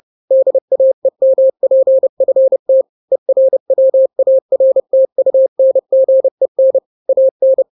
Morse code, heard as a keyed tone, about 30 wpm, 530 Hz. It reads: DAEMPFT ERWARTUNGEN AN
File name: daily / 29wpm